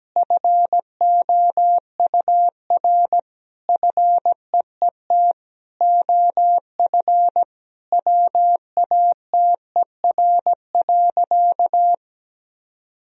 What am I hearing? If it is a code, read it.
Morse code, heard as a keyed tone, about 17 wpm, 700 Hz. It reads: FOUR FEET OF WATER.
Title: Sentences from Top 1000 Words - Review Part 1 - 17wpm